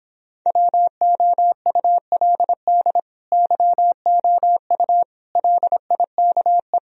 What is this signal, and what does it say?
Morse code, heard as a keyed tone, about 26 wpm, 700 Hz. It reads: WOULD YOU LIKE
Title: Speed-Racing - Sets of 4 Words 22wpm